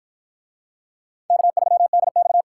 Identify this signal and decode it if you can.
Morse code, heard as a keyed tone, about 53 wpm, 700 Hz. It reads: K3DX